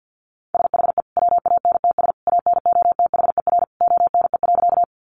Morse code, heard as a keyed tone, about 50 wpm, 700 Hz. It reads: SHE WANTS ANOTHER ONE.
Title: Easy Phrases 50wpm